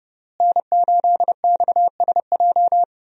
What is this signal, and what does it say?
Morse code, heard as a keyed tone, about 30 wpm, 700 Hz. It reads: N8XSJ
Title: USA Call Signs - 5 Characters - Repeated 3x - 30wpm